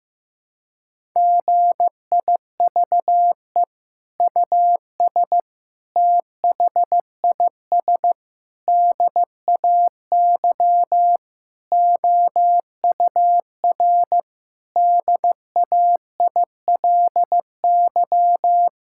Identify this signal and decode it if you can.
Morse code, heard as a keyed tone, about 15 wpm, 700 Hz. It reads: GIVE US THIS DAY OUR DAILY
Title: Sentences from Top 1600 Words 15wpm